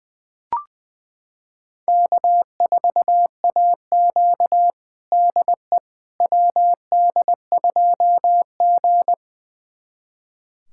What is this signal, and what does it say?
Morse code, heard as a keyed tone, about 21 wpm, 700 Hz. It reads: K4AQ DE WD2G
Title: USA Callsign DE Callsign - 4 Characters - 20wpm